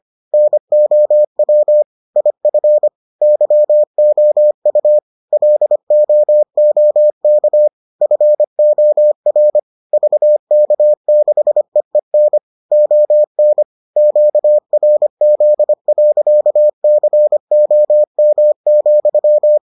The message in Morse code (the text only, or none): NOW IF YOU LOOK FOR VK6EEN ON QRZ.COM,